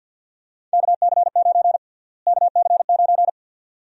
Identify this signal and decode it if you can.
Morse code, heard as a keyed tone, about 51 wpm, 700 Hz. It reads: KC9 KC9
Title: USA Call Sign Prefixes - Repeated 2x - 50wpm